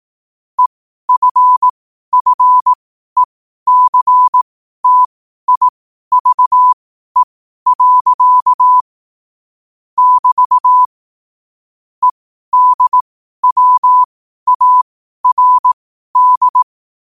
Morse code, heard as a keyed tone, about 18 wpm, spaced as for 12 wpm, 1000 Hz. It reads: EFFECTIVE. <BT> EDWARD
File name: qotdfiles / 12wpm